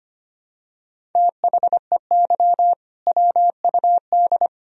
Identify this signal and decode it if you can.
Morse code, heard as a keyed tone, about 25 wpm, 700 Hz. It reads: THEY WUD
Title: Sentences from Top 500 Words and Abbreviations 25wpm